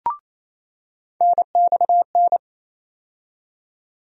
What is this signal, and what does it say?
Morse code, heard as a keyed tone, about 28 wpm, 700 Hz. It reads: NXN
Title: USA Call Sign Postfixes - 3 Letters 28wpm